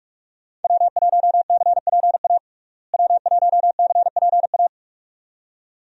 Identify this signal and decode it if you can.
Morse code, heard as a keyed tone, about 44 wpm, 700 Hz. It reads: W1CPA W1CPA
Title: USA Call Signs - 5 Characters - Repeated 2x - 45wpm